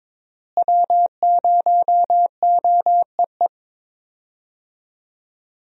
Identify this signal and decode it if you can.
Morse code, heard as a keyed tone, about 22 wpm, 700 Hz. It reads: W0OEE
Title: USA Call Signs - 5 Characters - Repeated 3x - 22wpm